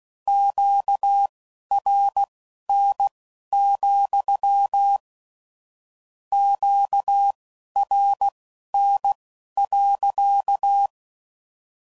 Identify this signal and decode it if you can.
Morse code, heard as a keyed tone, about 16 wpm, spaced as for 11 wpm, 800 Hz. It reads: QRN, QRN.
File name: CW Code Groups @15WPM / Q Calls / Q Calls sppc